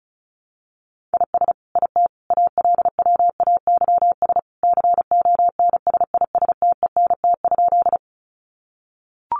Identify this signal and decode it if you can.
Morse code, heard as a keyed tone, about 34 wpm, 700 Hz. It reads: IS IT ALWAYS CONSISTENT?
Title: Sentences from Top 1500 Words 35wpm